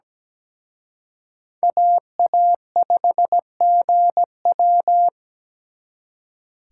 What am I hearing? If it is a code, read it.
Morse code, heard as a keyed tone, about 20 wpm, 700 Hz. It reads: AA5GW
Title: Speed Racing - USA Call Signs 17wpm